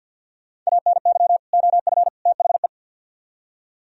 Morse code, heard as a keyed tone, about 62 wpm, 700 Hz. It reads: ANY OF THE
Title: Speed-Racing - Sets of 3 Words 50wpm